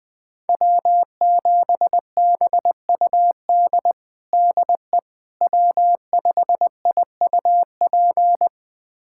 Morse code, heard as a keyed tone, about 21 wpm, 700 Hz. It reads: W7BUD DE W5IUP
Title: USA Callsign DE Callsign - 5 Characters - 20wpm